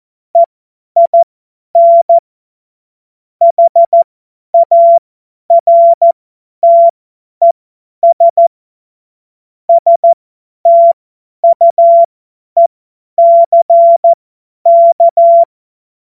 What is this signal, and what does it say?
Morse code, heard as a keyed tone, about 14 wpm, spaced as for 10 wpm, 680 Hz. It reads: EIN HARTES STUECK
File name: Morsekurs / Witze / Witz 03. Mai 2022 50 F70